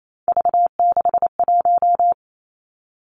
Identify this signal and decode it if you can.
Morse code, heard as a keyed tone, about 28 wpm, 700 Hz. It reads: V61
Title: Three Challenging Letter-Number Combinations 28wpm